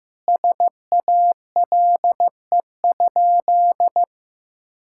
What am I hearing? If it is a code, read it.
Morse code, heard as a keyed tone, about 15 wpm, 700 Hz. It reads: SALE?
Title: Sentences from Top 1600 Words 15wpm